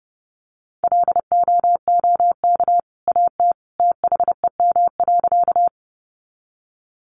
Morse code, heard as a keyed tone, about 30 wpm, 700 Hz. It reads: LOOK AT THEM.
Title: Sentences from Top 100 Words 30wpm